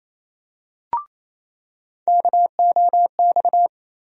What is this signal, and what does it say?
Morse code, heard as a keyed tone, about 28 wpm, 700 Hz. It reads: KOX
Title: USA Call Sign Postfixes - 3 Letters 28wpm